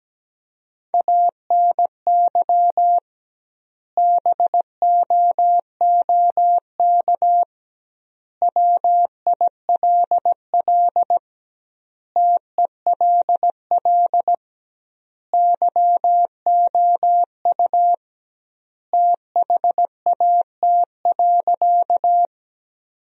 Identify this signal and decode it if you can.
Morse code, heard as a keyed tone, about 17 wpm, 700 Hz. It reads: ANY BOOK WILL TELL YOU THAT.
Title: 2x Word-Spacing - Sentences from Top 300 Words - 17wpm